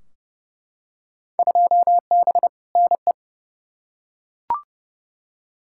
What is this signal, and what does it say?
Morse code, heard as a keyed tone, about 31 wpm, 700 Hz. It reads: 2B NE
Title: Field Day Exchanges Encore - 30wpm